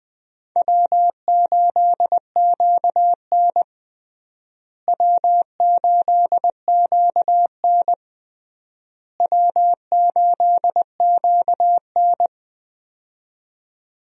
Morse code, heard as a keyed tone, about 21 wpm, 700 Hz. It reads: W8QN W8QN W8QN
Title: USA Call Signs - 4 Characters - Repeated 3x - 20wpm